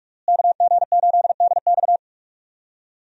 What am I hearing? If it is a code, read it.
Morse code, heard as a keyed tone, about 42 wpm, 700 Hz. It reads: KG8DX
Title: USA Call Signs - 5 Characters - Repeated 2x - 45wpm